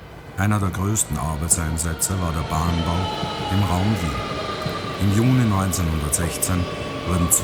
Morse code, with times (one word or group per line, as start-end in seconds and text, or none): none